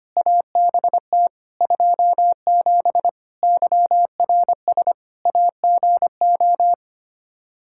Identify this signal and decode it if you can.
Morse code, heard as a keyed tone, about 25 wpm, 700 Hz. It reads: ABT 27 YRS AGO
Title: Practice QSO - Exchange 3 - More Info - 25wpm